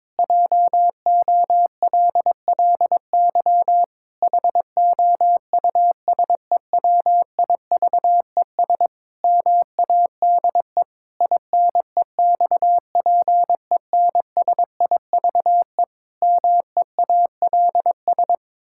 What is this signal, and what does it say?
Morse code, heard as a keyed tone, about 22 wpm, 700 Hz. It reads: JOLLY HOUSEWIVES MADE INEXPENSIVE MEALS
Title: Pangram Challenge v2 - 22wpm